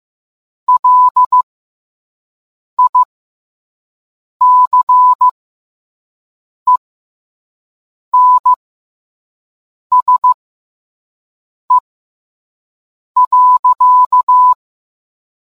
Morse code, heard as a keyed tone, about 15 wpm, spaced as for 5 wpm, 1000 Hz. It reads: LICENSE.